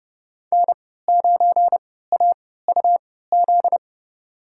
Morse code, heard as a keyed tone, about 31 wpm, spaced as for 17 wpm, 700 Hz. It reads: N9AUZ
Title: USA Call Signs - 5 Characters - 3x Character Spacing - 30wpm